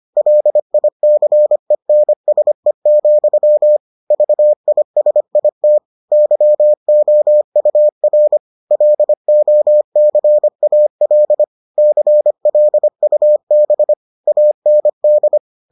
Morse code, heard as a keyed tone, about 25 wpm, 600 Hz. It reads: LICENSE, VISIT YOUR LOCAL CLUB AND